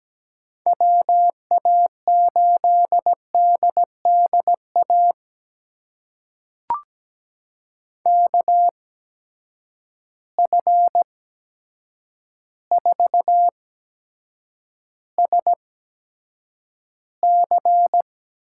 Morse code, heard as a keyed tone, about 17 wpm, spaced as for 8 wpm, 700 Hz. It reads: WA8DDA KF4SC
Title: USA Call Signs - 8x Character Spacing - 17wpm